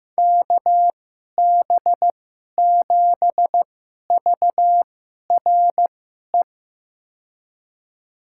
Morse code, heard as a keyed tone, about 15 wpm, spaced as for 11 wpm, 700 Hz. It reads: KB7VRE
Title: USA Call Signs - 6 Characters - 2x Character Spacing - 15wpm